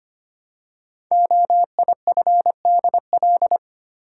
Morse code, heard as a keyed tone, about 25 wpm, 700 Hz. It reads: OIFDL
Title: Code Groups - 5 Letters - 25wpm